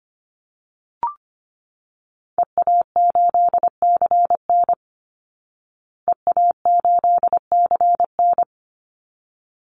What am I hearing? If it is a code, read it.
Morse code, heard as a keyed tone, about 25 wpm, 700 Hz. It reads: EA8CN EA8CN
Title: Call Signs Repeated 3x - 25wpm